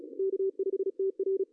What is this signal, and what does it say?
Morse code, heard as a keyed tone, about 32 wpm, 410 Hz. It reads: K5TR
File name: CW / cqww-cw-k5tr